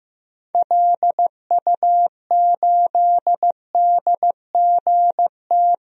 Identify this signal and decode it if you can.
Morse code, heard as a keyed tone, about 16 wpm, 700 Hz. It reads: LU8DGT
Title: DXCC-100 Call Signs - 6 Characters - 15wpm